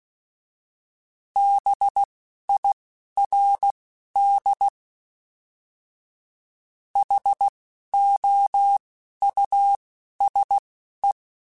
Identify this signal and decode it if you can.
Morse code, heard as a keyed tone, about 16 wpm, spaced as for 11 wpm, 800 Hz. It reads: BIRD HOUSE